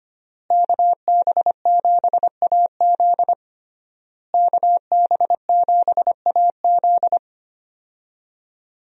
Morse code, heard as a keyed tone, about 25 wpm, 700 Hz. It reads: KB7AZ KB7AZ
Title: USA Call Signs - 5 Characters - Repeated 3x - 25wpm